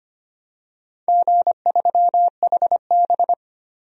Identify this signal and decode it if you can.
Morse code, heard as a keyed tone, about 25 wpm, 700 Hz. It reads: G3HB
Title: DXCC-100 Call Signs - 4 Characters - Repeated 2x - 25wpm